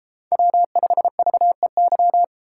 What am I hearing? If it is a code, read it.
Morse code, heard as a keyed tone, about 32 wpm, 700 Hz. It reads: W5VEY